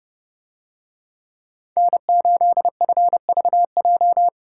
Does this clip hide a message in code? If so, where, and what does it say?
Yes: Morse code from 1.8 to 4.3 s, heard as a keyed tone, about 30 wpm, 700 Hz. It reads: N8FVJ